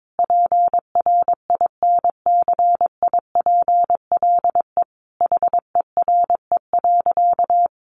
Morse code, heard as a keyed tone, about 22 wpm, 700 Hz. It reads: PRINCIPLE HERE.
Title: Sentences from Top 1400 Words 22wpm